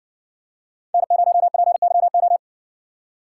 Morse code, heard as a keyed tone, about 60 wpm, 700 Hz. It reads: N0PYO